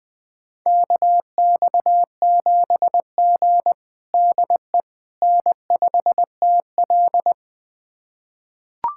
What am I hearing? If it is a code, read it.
Morse code, heard as a keyed tone, about 20 wpm, 700 Hz. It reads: KX7G DE N5TL
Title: USA Callsign DE Callsign - 4 Characters - 20wpm